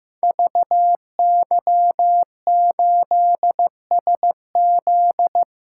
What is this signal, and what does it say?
Morse code, heard as a keyed tone, about 15 wpm, 700 Hz. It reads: VY8SZ
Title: Code Groups - 5 Letter:Numbers - 15wpm